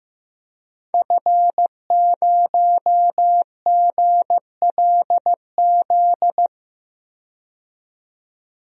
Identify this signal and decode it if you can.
Morse code, heard as a keyed tone, about 15 wpm, 700 Hz. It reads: F0GLZ
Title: DXCC-100 Call Signs - 5 Characters - 15wpm